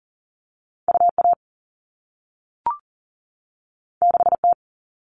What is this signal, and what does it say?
Morse code, heard as a keyed tone, about 41 wpm, 700 Hz. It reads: UA 6T
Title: All Cut Numbers - 2-Digit 40wpm